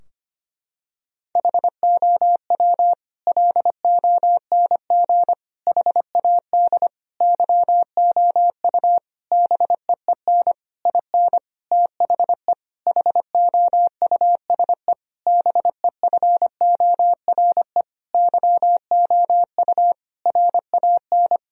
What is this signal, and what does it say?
Morse code, heard as a keyed tone, about 25 wpm, 700 Hz. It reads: HOW LONG HAD YOU BEEN IN THE HOUSE BEFORE YOU RAN
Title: Sentences from Top 1000 Words - Review Part 3 - 25wpm